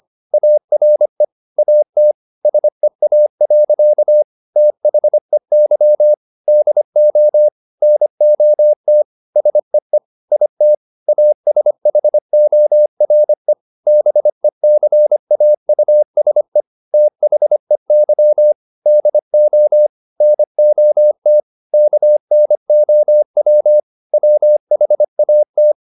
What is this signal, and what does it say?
Morse code, heard as a keyed tone, about 25 wpm, 600 Hz. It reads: ARE AT SEA. THEY DO NOT SEE IT ASHORE BECAUSE THEY DO NOT KNOW WHAT
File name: the old man and the sea part 1 25wpm